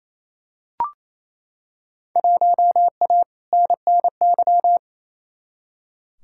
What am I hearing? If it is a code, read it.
Morse code, heard as a keyed tone, about 28 wpm, 700 Hz. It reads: 1A NNY
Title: Field Day Exchanges Encore - 28wpm